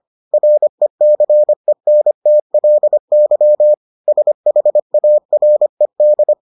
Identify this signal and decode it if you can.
Morse code, heard as a keyed tone, about 25 wpm, 600 Hz. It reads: RECENTLY SHARED